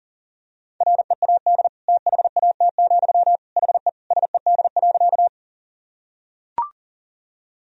Morse code, heard as a keyed tone, about 40 wpm, 700 Hz. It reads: READ THAT, HE SED.
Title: Sentences from Top 200 Words and Abbreviations 40wpm